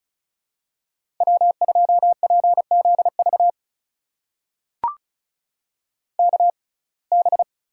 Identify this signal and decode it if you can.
Morse code, heard as a keyed tone, about 36 wpm, spaced as for 26 wpm, 700 Hz. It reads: W2PZV KB
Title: USA Call Signs - 6x Character Spacing - 35wpm